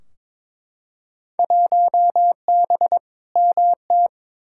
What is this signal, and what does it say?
Morse code, heard as a keyed tone, about 22 wpm, 700 Hz. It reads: 1B MT
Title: Field Day Exchanges - 22wpm